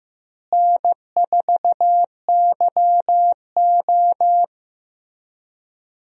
Morse code, heard as a keyed tone, about 15 wpm, 700 Hz. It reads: N4YO
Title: USA Call Signs - 4 Characters - Repeated 3x - 15wpm